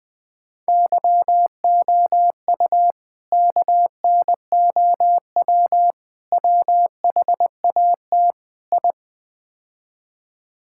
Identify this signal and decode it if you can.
Morse code, heard as a keyed tone, about 20 wpm, 700 Hz. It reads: YOU KNOW WHAT I
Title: Build A Phrase - 5-Word Phrases - 20wpm